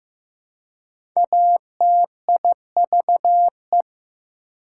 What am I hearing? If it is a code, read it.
Morse code, heard as a keyed tone, about 15 wpm, 700 Hz. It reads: ATIVE
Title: Prefixes and Suffixes 15wpm